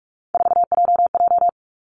Morse code, heard as a keyed tone, about 46 wpm, 700 Hz. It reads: 4JJ
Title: Three Challenging Letter-Number Combinations 45wpm